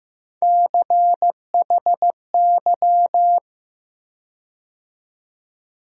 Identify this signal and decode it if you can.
Morse code, heard as a keyed tone, about 15 wpm, 700 Hz. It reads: CHY
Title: USA Call Sign Postfixes - 3 Letters 15wpm